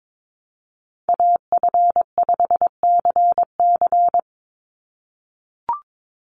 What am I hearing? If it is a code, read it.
Morse code, heard as a keyed tone, about 22 wpm, 700 Hz. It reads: AF5CC
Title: Call Signs - 22wpm